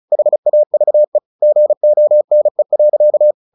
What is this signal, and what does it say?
Morse code, heard as a keyed tone, about 34 wpm, 600 Hz. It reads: HAVE GONE.
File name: the old man and the sea / the old man and the sea 1 35wpm